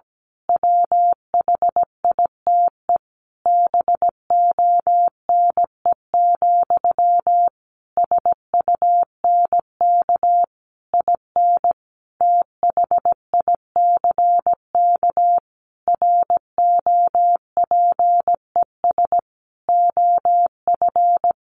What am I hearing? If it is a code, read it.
Morse code, heard as a keyed tone, about 17 wpm, 700 Hz. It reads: WHITE BONE, SUNK IN THICK ROPES OF
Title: A Sound Of Thunder 17wpm